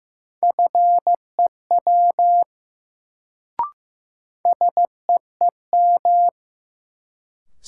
Morse code, heard as a keyed tone, about 16 wpm, 700 Hz. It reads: FEW SEEM
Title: Sentences from Top 300 Words - New Word Drill - 15wpm